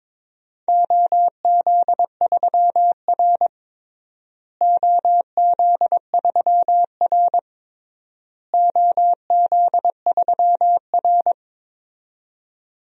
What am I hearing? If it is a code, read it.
Morse code, heard as a keyed tone, about 22 wpm, 700 Hz. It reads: OZ3R OZ3R OZ3R
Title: DXCC-100 Call Signs - 4 Characters - Repeated 3x - 22wpm